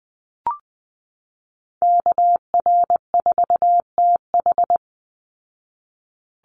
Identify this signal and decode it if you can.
Morse code, heard as a keyed tone, about 20 wpm, 700 Hz. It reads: KR4TH
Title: Call Signs Repeated 3x - 20wpm